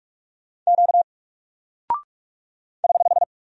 Morse code, heard as a keyed tone, about 45 wpm, 700 Hz. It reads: Q <HH>
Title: Single Character (Letter:Number:Punctuation:Prosign) Alt 45wpm